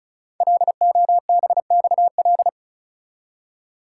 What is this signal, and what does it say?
Morse code, heard as a keyed tone, about 36 wpm, 700 Hz. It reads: LOBXL
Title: Code Groups - 5 Letters - 35wpm